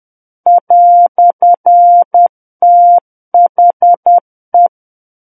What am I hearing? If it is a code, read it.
Morse code, heard as a keyed tone, about 10 wpm, 700 Hz. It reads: "THE
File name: Book One / 01-Chapter 1 - 10 WPM